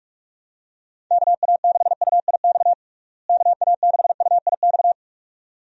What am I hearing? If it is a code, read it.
Morse code, heard as a keyed tone, about 46 wpm, 700 Hz. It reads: KA6UIX KA6UIX
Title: USA Call Signs - 6 Characters - Repeated 2x - 45wpm